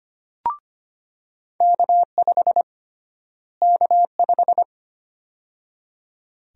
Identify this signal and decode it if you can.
Morse code, heard as a keyed tone, about 25 wpm, 700 Hz. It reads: K5 K5
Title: USA Call Sign Prefixes - Repeated 2x - 25wpm